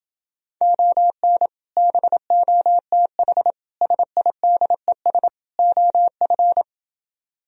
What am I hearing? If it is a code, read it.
Morse code, heard as a keyed tone, about 29 wpm, 700 Hz. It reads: ON BOTH SIDES OF